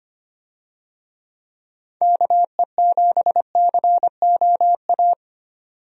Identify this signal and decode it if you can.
Morse code, heard as a keyed tone, about 25 wpm, 700 Hz. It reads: KE7COA
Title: USA Call Signs - 6 Characters - 25wpm